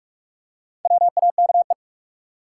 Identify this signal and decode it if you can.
Morse code, heard as a keyed tone, about 46 wpm, 700 Hz. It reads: WAKE
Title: Four Letter Words - 45wpm